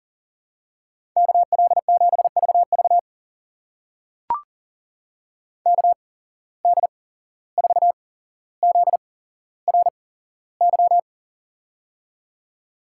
Morse code, heard as a keyed tone, about 42 wpm, spaced as for 16 wpm, 700 Hz. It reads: KL7VV KD4ZRY
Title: USA Call Signs - 8x Character Spacing - 40wpm